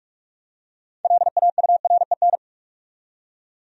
Morse code, heard as a keyed tone, about 46 wpm, 700 Hz. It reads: LAUREN
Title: USA Common Names - 45wpm